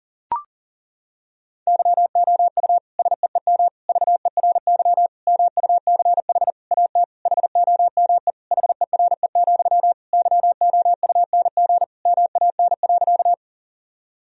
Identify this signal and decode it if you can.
Morse code, heard as a keyed tone, about 40 wpm, 700 Hz. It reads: YOU SEEM VERY MUCH AT HOME HERE, YOUNG MAN.
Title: Sentences from Top 1000 Words - Review Part 1 - 40wpm